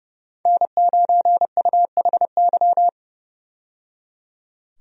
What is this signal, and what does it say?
Morse code, heard as a keyed tone, about 32 wpm, 700 Hz. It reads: N9UHY
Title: USA Call Signs - 5 Characters - 30wpm